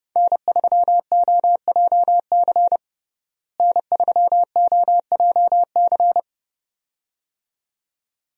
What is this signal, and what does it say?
Morse code, heard as a keyed tone, about 32 wpm, 700 Hz. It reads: N3OJC N3OJC
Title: USA Call Signs - 5 Characters - Repeated 2x - 30wpm